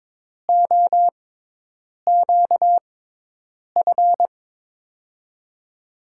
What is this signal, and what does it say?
Morse code, heard as a keyed tone, about 23 wpm, spaced as for 8 wpm, 700 Hz. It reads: OQF
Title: USA Call Signs - 6 Characters - 6x Character Spacing - 22wpm